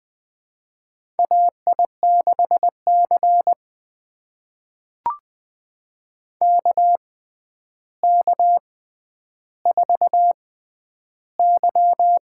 Morse code, heard as a keyed tone, about 20 wpm, spaced as for 10 wpm, 700 Hz. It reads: AI6C KK4Y
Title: USA Call Signs - 6x Character Spacing - 20wpm